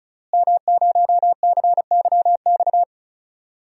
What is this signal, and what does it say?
Morse code, heard as a keyed tone, about 36 wpm, 700 Hz. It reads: M0CYX
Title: DXCC-100 Call Signs - 5 Characters - Repeated 3x - 35wpm